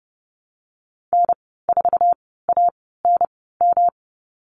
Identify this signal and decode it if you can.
Morse code, heard as a keyed tone, about 30 wpm, spaced as for 17 wpm, 700 Hz. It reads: N4ANM
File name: Lesson 19g - Letter w - Call Signs 30wpm